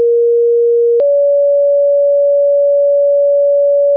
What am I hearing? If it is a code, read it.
Two-tone page, heard as a one-second tone, then a three-second tone, about 470 then 570 Hz. It is heard from 0.0 to 4.0 s.